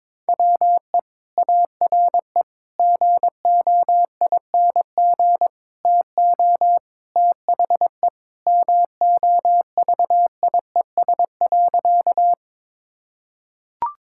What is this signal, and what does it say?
Morse code, heard as a keyed tone, about 22 wpm, 700 Hz. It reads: WE ARE GOING TO THE MOVIES.
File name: practice / 22wpm / Sentences with 4-Word Phrases - 22wpm